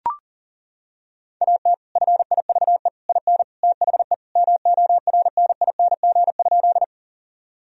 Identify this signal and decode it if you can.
Morse code, heard as a keyed tone, about 40 wpm, 700 Hz. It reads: AT FIVE IN THE MORNING?
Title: Sentences from Top 1000 Words - Review Part 2 - 40wpm